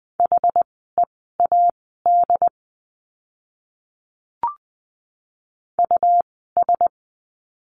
Morse code, heard as a keyed tone, about 20 wpm, spaced as for 15 wpm, 700 Hz. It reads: HEAD US
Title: 101 Lesson 17f - Letter u - Words 20wpm